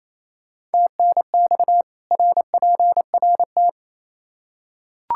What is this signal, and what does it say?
Morse code, heard as a keyed tone, about 28 wpm, 700 Hz. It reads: TNX RPRT
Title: K1USN SST QSO Practice 28wpm